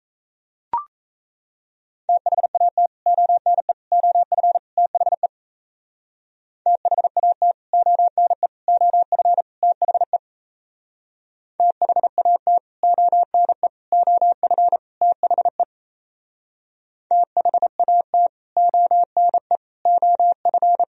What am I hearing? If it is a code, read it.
Morse code, heard as a keyed tone, about 36 wpm, 700 Hz. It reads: THAT ONE OF THE THAT ONE OF THE THAT ONE OF THE THAT ONE OF